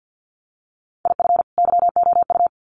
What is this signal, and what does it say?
Morse code, heard as a keyed tone, about 50 wpm, 700 Hz. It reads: IF YOU
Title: Sets of 2 Words 50wpm